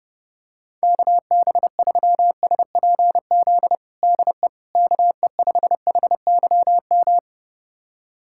Morse code, heard as a keyed tone, about 30 wpm, 700 Hz. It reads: KB3SPZ DE KE5HYM
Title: USA Callsign DE Callsign - 6 Characters - 30wpm